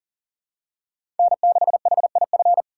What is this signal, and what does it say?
Morse code, heard as a keyed tone, about 37 wpm, 700 Hz. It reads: N6HIF
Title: USA Call Signs - 5 Characters - 4x Character Spacing - 40wpm